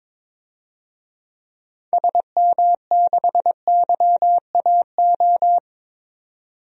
Morse code, heard as a keyed tone, about 22 wpm, 700 Hz. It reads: SM6YAO
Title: DXCC-100 Call Signs - 6 Characters - 22wpm